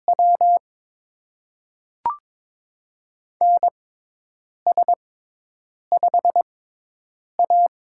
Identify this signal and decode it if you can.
Morse code, heard as a keyed tone, about 22 wpm, spaced as for 8 wpm, 700 Hz. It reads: W NS5A
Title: USA Call Signs - 4 Characters - 6x Character Spacing - 22wpm